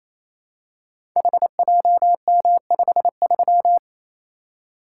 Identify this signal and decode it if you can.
Morse code, heard as a keyed tone, about 28 wpm, 700 Hz. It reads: HJM53
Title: Code Groups - 5 Letter:Numbers - 28wpm